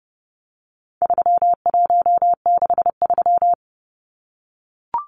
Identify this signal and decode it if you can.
Morse code, heard as a keyed tone, about 30 wpm, 700 Hz. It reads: 3163
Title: Numbers - 4-Digit 30wpm